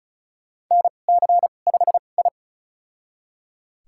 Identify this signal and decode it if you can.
Morse code, heard as a keyed tone, about 34 wpm, spaced as for 25 wpm, 700 Hz. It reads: NC5I
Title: USA Call Signs - 4 Characters - 2x Character Spacing - 35wpm